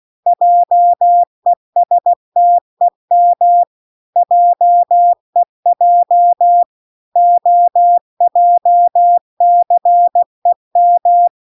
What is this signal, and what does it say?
Morse code, heard as a keyed tone, about 16 wpm, 700 Hz. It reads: JESTEM JEJ OJCEM